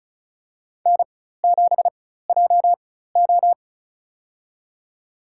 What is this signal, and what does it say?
Morse code, heard as a keyed tone, about 36 wpm, spaced as for 16 wpm, 700 Hz. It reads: N7JO
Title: USA Call Signs - 4 Characters - 4x Character Spacing - 35wpm